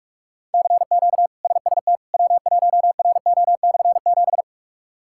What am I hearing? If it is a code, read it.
Morse code, heard as a keyed tone, about 46 wpm, 700 Hz. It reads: CQ SST W1RO/7
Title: K1USN SST QSO Practice - 45wpm